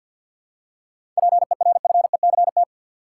Morse code, heard as a keyed tone, about 52 wpm, 700 Hz. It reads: PERFECT
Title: Seven Letter Words - 50wpm